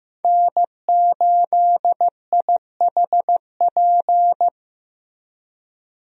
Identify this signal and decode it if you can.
Morse code, heard as a keyed tone, about 15 wpm, 700 Hz. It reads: N8IHP
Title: USA Call Signs - 5 Characters - Repeated 3x - 15wpm